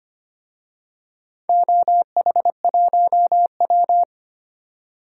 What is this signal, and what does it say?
Morse code, heard as a keyed tone, about 25 wpm, 700 Hz. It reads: OH1W
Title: DXCC-100 Call Signs - 4 Characters - 25wpm